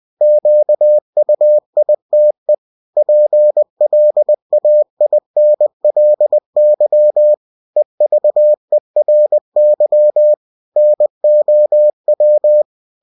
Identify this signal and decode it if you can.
Morse code, heard as a keyed tone, about 20 wpm, 600 Hz. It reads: QUITE PLAINLY EVERY NOW
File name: cw-books / winnie-the-pooh / winnie-20wpm-page-0024